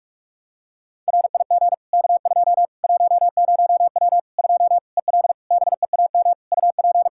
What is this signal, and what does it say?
Morse code, heard as a keyed tone, about 45 wpm, 700 Hz. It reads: RIG K2 10W 2 EL BEAM UP
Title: Practice QSO - Exchange 2 - Info - 45wpm